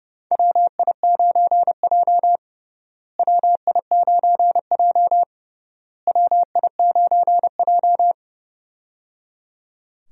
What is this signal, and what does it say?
Morse code, heard as a keyed tone, about 30 wpm, 700 Hz. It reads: WI9J WI9J WI9J
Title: USA Call Signs - 4 Characters - Repeated 3x - 30wpm